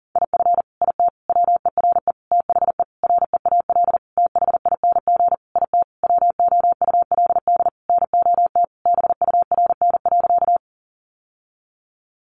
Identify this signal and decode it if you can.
Morse code, heard as a keyed tone, about 40 wpm, 700 Hz. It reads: IF IT WERE THE REAL THING IT WOULD NOT BURN.
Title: Sentences from Top 1300 Words 40wpm